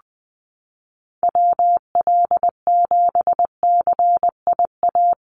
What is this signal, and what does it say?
Morse code, heard as a keyed tone, about 20 wpm, 700 Hz. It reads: WL7CIA
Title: USA Call Signs Repeated 2x 20wpm